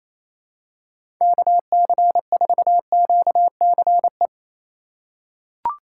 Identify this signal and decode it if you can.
Morse code, heard as a keyed tone, about 28 wpm, 700 Hz. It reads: KC4QCE
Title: USA Call Signs - 6 Characters - 2x Character Spacing - 28wpm